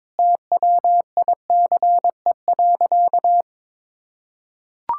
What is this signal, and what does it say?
Morse code, heard as a keyed tone, about 22 wpm, 700 Hz. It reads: TWICE.
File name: practice / 22wpm / Sentences with 3-Word Phrases - 22wpm